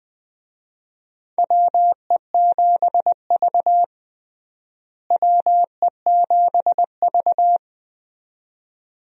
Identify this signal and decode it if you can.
Morse code, heard as a keyed tone, about 21 wpm, 700 Hz. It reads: WE7V WE7V
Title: USA Call Signs - 4 Characters - Repeated 2x - 20wpm